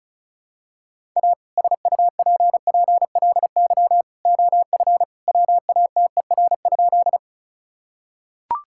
Morse code, heard as a keyed tone, about 36 wpm, 700 Hz. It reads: A SUPPLY OF WATER?
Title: Sentences from Top 1000 Words - Review Part 5 - 35wpm